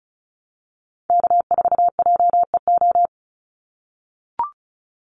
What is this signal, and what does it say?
Morse code, heard as a keyed tone, about 34 wpm, 700 Hz. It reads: K4JEO
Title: USA Call Signs 35wpm